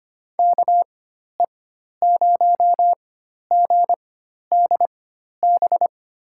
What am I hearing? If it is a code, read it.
Morse code, heard as a keyed tone, about 25 wpm, spaced as for 12 wpm, 700 Hz. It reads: KE0GDB
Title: USA Call Signs - 6 Characters - 4x Character Spacing - 25wpm